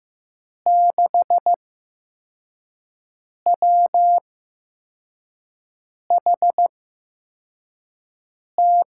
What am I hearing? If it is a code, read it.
Morse code, heard as a keyed tone, about 15 wpm, spaced as for 4 wpm, 700 Hz. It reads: 6WHT
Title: USA Call Signs - 8x Character Spacing - 15wpm